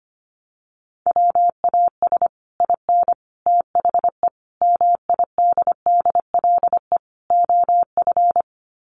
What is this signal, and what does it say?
Morse code, heard as a keyed tone, about 25 wpm, 700 Hz. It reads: WAS IN THE MIDDLE OF
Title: Sets of 5 Words Encore 25wpm